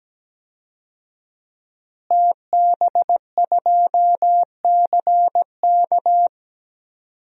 Morse code, heard as a keyed tone, about 17 wpm, 700 Hz. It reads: TB2CK
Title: DXCC-100 Call Signs - 5 Characters - Repeated 2x - 17wpm